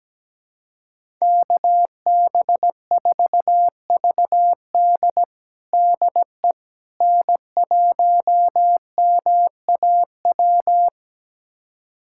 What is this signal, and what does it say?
Morse code, heard as a keyed tone, about 17 wpm, 700 Hz. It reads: KB4VD DE N1MAW
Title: USA Callsign DE Callsign - 5 Characters - 17wpm